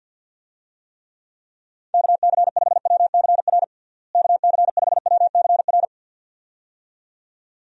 Morse code, heard as a keyed tone, about 52 wpm, 700 Hz. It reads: KC5WCR KC5WCR
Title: USA Call Signs - 6 Characters - Repeated 2x - 50wpm